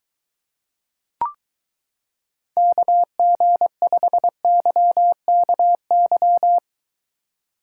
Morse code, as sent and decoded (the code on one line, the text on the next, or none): -.- --. ..... -.-- -.- -.--
KG5YKY